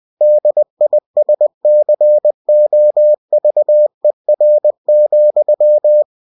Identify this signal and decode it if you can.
Morse code, heard as a keyed tone, about 20 wpm, 600 Hz. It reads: DISCOVER,
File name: cw-books / winnie-the-pooh / winnie-20wpm-page-0081